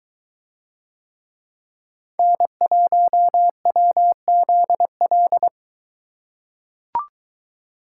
Morse code, heard as a keyed tone, about 22 wpm, 700 Hz. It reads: N1WZL